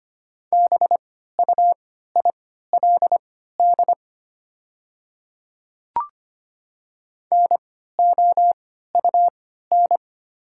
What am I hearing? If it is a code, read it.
Morse code, heard as a keyed tone, about 25 wpm, spaced as for 14 wpm, 700 Hz. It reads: BUILD NOUN
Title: 3x Character Spacing - Top 500 Words - 25wpm